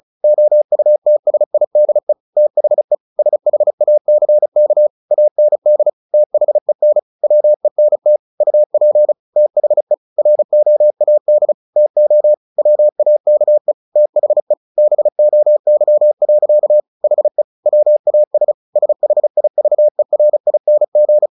Morse code, heard as a keyed tone, about 35 wpm, 600 Hz. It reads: OUTSIDE THE SHACK AND THEN WENT UP THE ROAD TO WAKE THE BOY. HE WAS SHIVERING
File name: the old man and the sea / the old man and the sea 1 35wpm